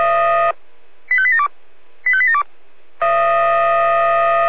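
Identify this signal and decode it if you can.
Five-tone selective call (ZVEI), heard as quick runs of five tones, tones of about 70 ms. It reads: 85872 85872
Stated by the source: ZVEI five-tone call 85252, received on SDR.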